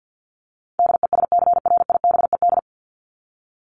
Morse code, heard as a keyed tone, about 50 wpm, 700 Hz. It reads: DESCRIBED
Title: ED Suffix - Top 100 Words with Base Words 50wpm